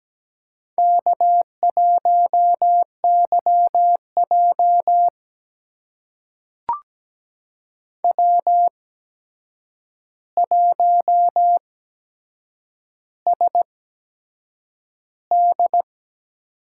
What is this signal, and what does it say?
Morse code, heard as a keyed tone, about 17 wpm, spaced as for 7 wpm, 700 Hz. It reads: K1YJ W1SD
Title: USA Call Signs - 4 Characters - 8x Character Spacing - 17wpm